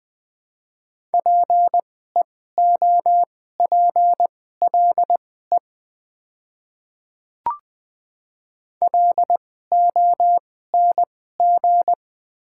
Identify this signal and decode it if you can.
Morse code, heard as a keyed tone, about 20 wpm, spaced as for 14 wpm, 700 Hz. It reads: PEOPLE LONG
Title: 2x Character Spacing - Top 100 Words - 20wpm